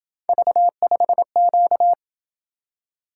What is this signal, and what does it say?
Morse code, heard as a keyed tone, about 26 wpm, 700 Hz. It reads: V5Q